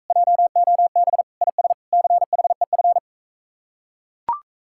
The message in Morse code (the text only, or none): JOB IS CHEF